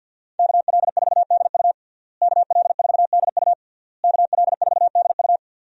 Morse code, heard as a keyed tone, about 51 wpm, 700 Hz. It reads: KL4DU KL4DU KL4DU
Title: USA Call Signs - 5 Characters - Repeated 3x - 50wpm